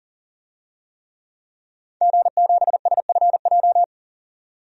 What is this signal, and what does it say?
Morse code, heard as a keyed tone, about 40 wpm, 700 Hz. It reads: G7SFJ
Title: DXCC-100 Call Signs - 5 Characters - 40wpm